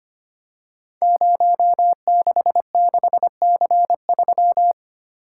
Morse code, heard as a keyed tone, about 25 wpm, 700 Hz. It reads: 066C3
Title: Code Groups - 5 Letter:Numbers - 25wpm